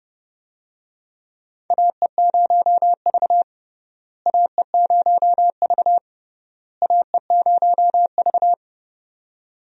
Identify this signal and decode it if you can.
Morse code, heard as a keyed tone, about 32 wpm, 700 Hz. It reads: AE0V AE0V AE0V
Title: USA Call Signs - 4 Characters - Repeated 3x - 30wpm